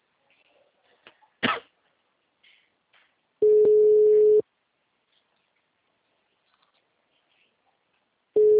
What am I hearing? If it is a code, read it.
Ringback tone (Europe, 425 Hz).